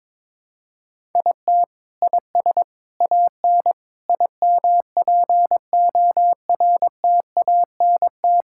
Morse code, heard as a keyed tone, about 22 wpm, 700 Hz. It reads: IT IS AN IMPORTANT
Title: Sentences from Top 1000 Words - Review Part 6 - 22wpm